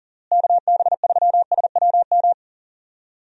Morse code, heard as a keyed tone, about 40 wpm, 700 Hz. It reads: KB3SWM
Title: USA Call Signs - 6 Characters - Repeated 3x - 40wpm